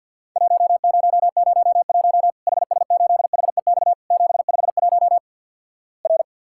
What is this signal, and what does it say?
Morse code, heard as a keyed tone, about 48 wpm, 700 Hz. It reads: 1001 HS8HEX 751 R
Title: Worked All Europe DX Contest - QTC Exchange - 50wpm